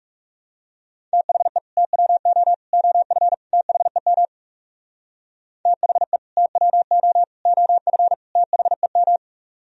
Morse code, heard as a keyed tone, about 40 wpm, 700 Hz. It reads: THE TWO OF THEM THE TWO OF THEM